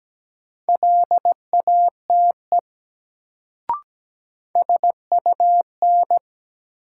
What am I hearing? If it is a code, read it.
Morse code, heard as a keyed tone, about 17 wpm, 700 Hz. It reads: LATE SUN
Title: Sentences from Top 300 Words - New Word Drill - 17wpm